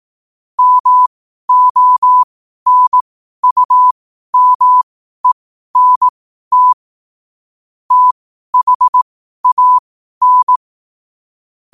Morse code, heard as a keyed tone, about 18 wpm, spaced as for 12 wpm, 1000 Hz. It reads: MONUMENT THAN
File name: qotdfiles / 12wpm